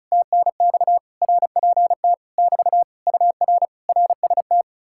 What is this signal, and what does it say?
Morse code, heard as a keyed tone, about 46 wpm, spaced as for 41 wpm, 700 Hz. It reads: TNX RPT <BT> UR RST
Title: Speed-Racing Common QSO Phrases - 35wpm